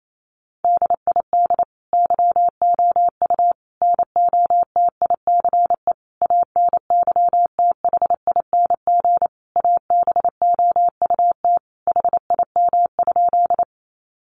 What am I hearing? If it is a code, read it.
Morse code, heard as a keyed tone, about 28 wpm, 700 Hz. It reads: DID YOU NOTICE ANYTHING ABOUT HIM?
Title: Sentences from Top 1000 Words 28wpm